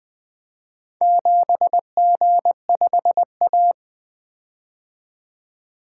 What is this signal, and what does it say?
Morse code, heard as a keyed tone, about 20 wpm, 700 Hz. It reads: 7G5A
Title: Four Letter-Number Combinations - 20wpm